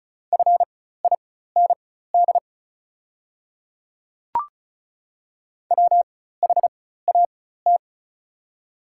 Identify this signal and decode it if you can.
Morse code, heard as a keyed tone, about 36 wpm, spaced as for 16 wpm, 690 Hz. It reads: FIND WHAT
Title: 4x Character Spacing - Top 200 Words - 35wpm